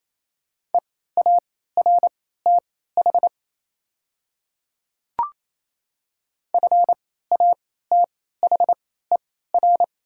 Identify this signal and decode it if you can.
Morse code, heard as a keyed tone, about 28 wpm, spaced as for 16 wpm, 700 Hz. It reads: EARTH FATHER
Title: 3x Character Spacing - Top 200 Words - 28wpm